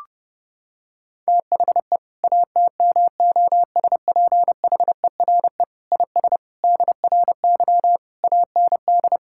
Morse code, heard as a keyed tone, about 32 wpm, 700 Hz. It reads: THE ATMOSPHERE IS DRY AND
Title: Sentences from Top 1600 Words 30wpm